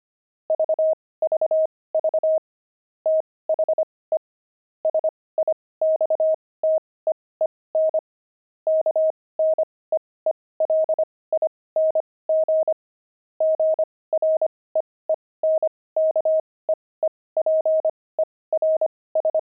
Morse code, heard as a keyed tone, about 25 wpm, spaced as for 18 wpm, 630 Hz. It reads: VVV THE SIXTEEN KNEELING GREENKEEPERS
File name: pangram-fs-18-cs-25-beehive